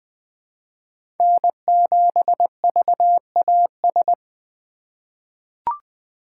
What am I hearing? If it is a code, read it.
Morse code, heard as a keyed tone, about 20 wpm, 700 Hz. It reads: N7VAS
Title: USA Call Signs - 5 Characters - 4x Character Spacing - 20wpm